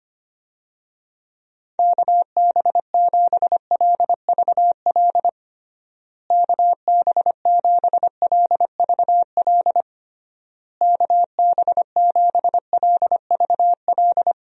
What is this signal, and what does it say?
Morse code, heard as a keyed tone, about 25 wpm, 700 Hz. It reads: KB7LVL KB7LVL KB7LVL
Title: USA Call Signs - 6 Characters - Repeated 3x - 25wpm